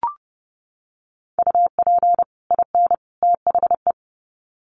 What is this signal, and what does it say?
Morse code, heard as a keyed tone, about 30 wpm, 700 Hz. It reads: UP IN THE
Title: Sets of 3 Words 30wpm